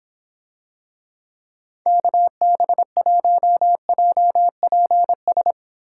Morse code, heard as a keyed tone, about 26 wpm, 700 Hz. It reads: KB1JPS